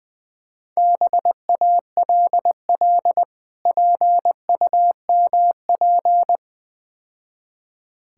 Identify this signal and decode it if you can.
Morse code, heard as a keyed tone, about 20 wpm, 690 Hz. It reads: BALL PUMP
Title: Difficult Words - Speed-Racing with Hints - 20wpm